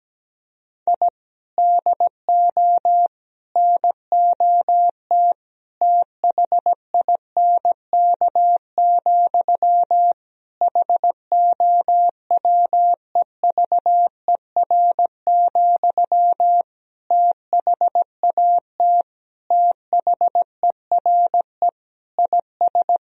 Morse code, heard as a keyed tone, about 17 wpm, 700 Hz. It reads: I DO NOT THINK, HOWEVER, THAT THERE IS
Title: Sentences from Top 1000 Words - Review Part 3 - 17wpm